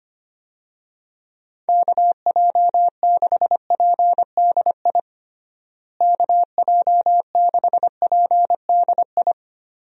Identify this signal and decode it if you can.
Morse code, heard as a keyed tone, about 25 wpm, 700 Hz. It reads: KJ6PDI KJ6PDI
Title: USA Call Signs - 6 Characters - Repeated 3x - 25wpm